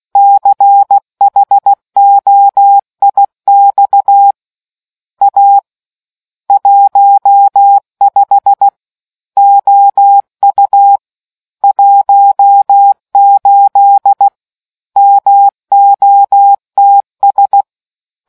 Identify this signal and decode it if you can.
Morse code, heard as a keyed tone, about 16 wpm, 800 Hz. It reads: CHOIX A 15 OU 18 MOTS